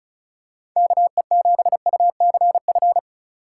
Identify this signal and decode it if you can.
Morse code, heard as a keyed tone, about 36 wpm, 700 Hz. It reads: KE7UCF
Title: USA Call Signs - 6 Characters - 6x Character Spacing - 35wpm